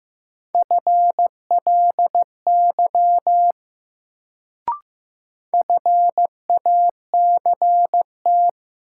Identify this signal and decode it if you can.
Morse code, heard as a keyed tone, about 15 wpm, 700 Hz. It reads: FLY FACT
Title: Sentences from Top 500 Words - New Word Drill - 15wpm